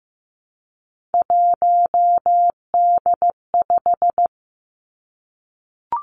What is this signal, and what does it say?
Morse code, heard as a keyed tone, about 15 wpm, 700 Hz. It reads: 1D5
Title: Three Challenging Letter-Number Combinations 15wpm